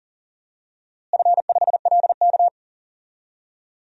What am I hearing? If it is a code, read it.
Morse code, heard as a keyed tone, about 40 wpm, 700 Hz. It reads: F5LK
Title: DXCC-100 Call Signs - 4 Characters - Repeated 3x - 40wpm